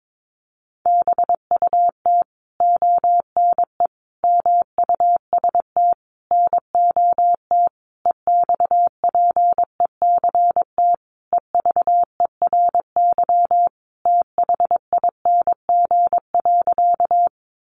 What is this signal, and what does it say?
Morse code, heard as a keyed tone, about 22 wpm, 700 Hz. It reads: BUT ONE MUST NOT EXPECT EVERYTHING.
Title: Sentences from Top 1100 Words 22wpm